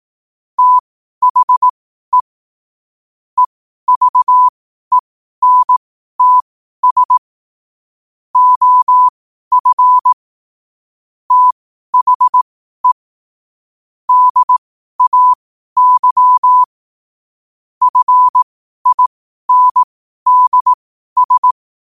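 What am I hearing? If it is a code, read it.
Morse code, heard as a keyed tone, about 18 wpm, spaced as for 12 wpm, 1000 Hz. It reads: THE EVENTS OF THE DAY FINDS
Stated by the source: Morse code quotes at 12 words per minute.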